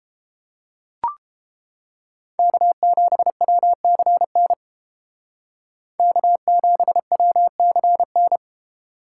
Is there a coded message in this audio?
Yes: Morse code from 2.4 to 8.4 s, heard as a keyed tone, about 31 wpm, 700 Hz. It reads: K7WCN K7WCN